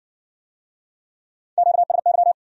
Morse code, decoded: QSY

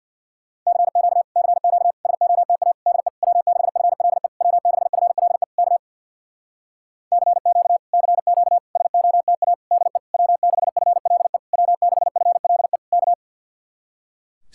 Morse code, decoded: CQ CQ SOTA DE W6FLE W6FLE K CQ CQ SOTA DE W6FLE W6FLE K

